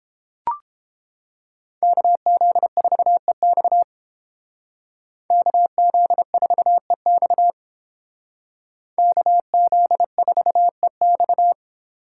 Morse code, heard as a keyed tone, about 30 wpm, 700 Hz. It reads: KZ4EX KZ4EX KZ4EX